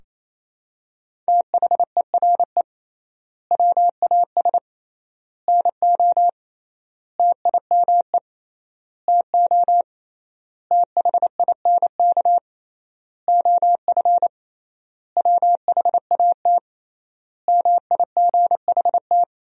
Morse code, heard as a keyed tone, about 28 wpm, 700 Hz. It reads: THERE WAS NO TIME TO THINK OF WHAT MIGHT
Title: 3x Word-Spacing - Sentences from Top 400 Words - 28wpm